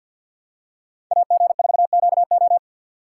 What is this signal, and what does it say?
Morse code, heard as a keyed tone, about 51 wpm, 700 Hz. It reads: AG4QO
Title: USA Call Signs - 5 Characters - 3x Character Spacing - 50wpm